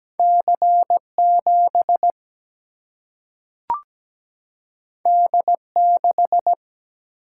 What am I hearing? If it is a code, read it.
Morse code, heard as a keyed tone, about 17 wpm, 700 Hz. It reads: C7 D6
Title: Single-Letter Single-Number 17wpm